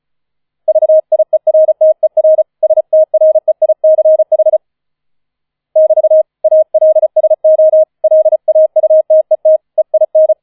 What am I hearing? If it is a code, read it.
Morse code, heard as a keyed tone, about 35 wpm, 630 Hz. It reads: VIERTER STREICH <BT> ALSO LAUTET EIN
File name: 004 Vierter Streich 35wpm